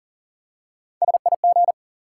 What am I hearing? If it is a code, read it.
Morse code, heard as a keyed tone, about 37 wpm, 700 Hz. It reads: SIG
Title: Top 180 QSO Elements 40wpm